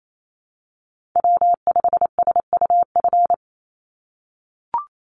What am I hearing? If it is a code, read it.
Morse code, heard as a keyed tone, about 28 wpm, 700 Hz. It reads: W5SUF
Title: USA Call Signs 28wpm